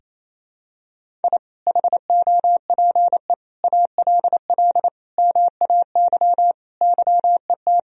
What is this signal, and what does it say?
Morse code, heard as a keyed tone, about 27 wpm, 700 Hz. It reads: I HOPE ALL MAY YET
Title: Sentences from Top 1000 Words - Review Part 1 - 28wpm